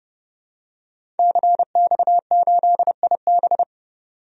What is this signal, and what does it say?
Morse code, heard as a keyed tone, about 32 wpm, 700 Hz. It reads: CX8IB
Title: DXCC-100 Call Signs - 5 Characters - 30wpm